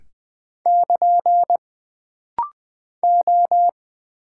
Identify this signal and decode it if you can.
Morse code, heard as a keyed tone, about 20 wpm, 700 Hz. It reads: <KN> O
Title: Single Character (Letter:Number:Punctuation:Prosign) - Rapid-Fire - 20wpm